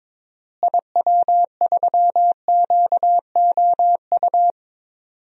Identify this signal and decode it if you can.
Morse code, heard as a keyed tone, about 22 wpm, 700 Hz. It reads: IW3QOU
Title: DXCC-100 Call Signs - 6 Characters - Repeated 3x - 22wpm